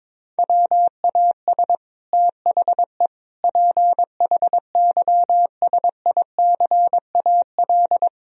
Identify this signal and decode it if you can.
Morse code, heard as a keyed tone, about 22 wpm, 700 Hz. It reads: WAS THE PHYSICAL
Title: Sentences from Top 1000 Words - Review Part 4 - 22wpm